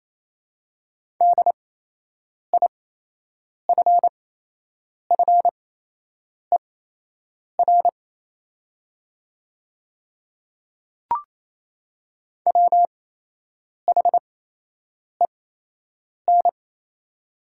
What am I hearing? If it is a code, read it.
Morse code, heard as a keyed tone, about 27 wpm, spaced as for 8 wpm, 700 Hz. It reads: DIFFER WHEN
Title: 8x Character Spacing - Top 200 Words - 28wpm